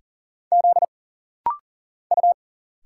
Morse code, heard as a keyed tone, about 41 wpm, 700 Hz. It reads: Z U
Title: Single Letters - Rapid-Fire 40wpm